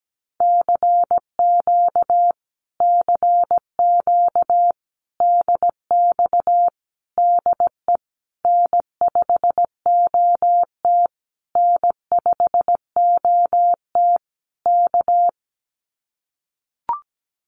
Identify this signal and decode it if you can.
Morse code, heard as a keyed tone, about 17 wpm, 700 Hz. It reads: CQ CQ DX DE N5OT N5OT K
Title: Common QSO Phrases Encore 17wpm